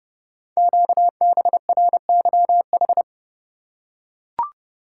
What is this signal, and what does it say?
Morse code, heard as a keyed tone, about 30 wpm, 690 Hz. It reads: QBRYH
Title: Code Groups - 5 Letters - 30wpm